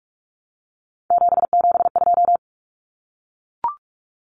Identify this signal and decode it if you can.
Morse code, heard as a keyed tone, about 44 wpm, 700 Hz. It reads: 772
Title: Numbers - 3-Digit 45wpm